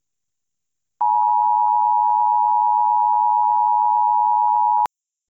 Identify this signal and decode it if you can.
Emergency Alert System (SAME) tones (the 853 and 960 Hz attention tone). It is heard from 1.0 to 4.9 s.